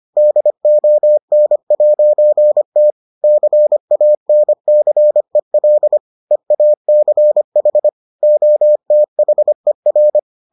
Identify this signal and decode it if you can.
Morse code, heard as a keyed tone, about 25 wpm, 600 Hz. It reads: DON'T CANCEL EACH OTHER